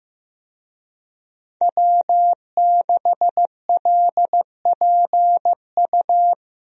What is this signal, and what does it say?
Morse code, heard as a keyed tone, about 15 wpm, 700 Hz. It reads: W6LPU
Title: USA Call Signs - 5 Characters - 15wpm